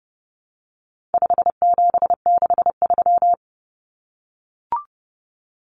Morse code, heard as a keyed tone, about 29 wpm, 700 Hz. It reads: 5763
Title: Common Cut Numbers - 4-Digit 30wpm